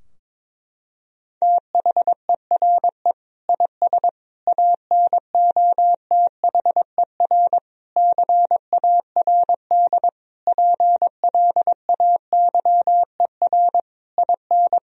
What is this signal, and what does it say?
Morse code, heard as a keyed tone, about 22 wpm, 700 Hz. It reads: THERE IS ANOTHER CARD PLAYER IN
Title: Sentences from Top 1000 Words - Review Part 4 - 22wpm